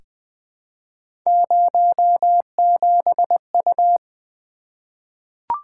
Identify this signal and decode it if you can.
Morse code, heard as a keyed tone, about 20 wpm, 700 Hz. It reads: 07U
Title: Three Letter-Number Combinations - 20wpm